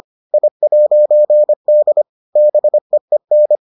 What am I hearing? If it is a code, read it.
Morse code, heard as a keyed tone, about 25 wpm, 600 Hz. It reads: I'D BEEN